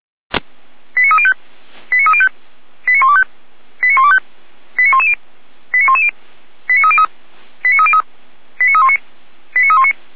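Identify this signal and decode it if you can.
Five-tone selective call (ZVEI), heard as quick runs of five tones, tones of about 70 ms. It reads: 89305 89305 89125 89125 89110 89110 89303 89303 89319 89319
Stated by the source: ZVEI five-tone call received on SDR, from audio.f85360.feuerwehr-orlamuende.org, Feuerwehr Orlamünde fire brigade.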